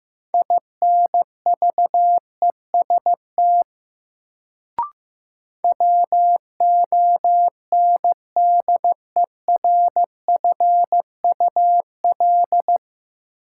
Morse code, heard as a keyed tone, about 15 wpm, 700 Hz. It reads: INVEST WONDERFUL
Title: Sentences from Top 1100 Words - New Word Drill Alt - 15wpm